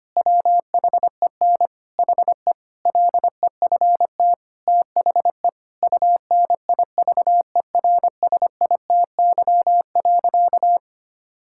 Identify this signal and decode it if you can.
Morse code, heard as a keyed tone, about 25 wpm, 700 Hz. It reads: WHEN HE LEFT THE UNIVERSITY.
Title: Sentences from Top 1000 Words - Review Part 5 - 25wpm